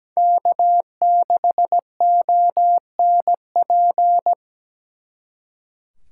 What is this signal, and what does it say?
Morse code, heard as a keyed tone, about 20 wpm, 700 Hz. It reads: K6ONP
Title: Speed Racing - USA 1x3 Call Signs 17wpm